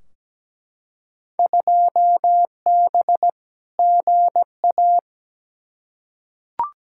Morse code, heard as a keyed tone, about 17 wpm, 700 Hz. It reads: 2B GA
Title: Field Day Exchanges Encore - 17wpm